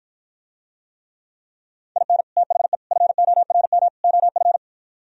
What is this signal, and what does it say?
Morse code, heard as a keyed tone, about 52 wpm, 700 Hz. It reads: IN THE FORM OF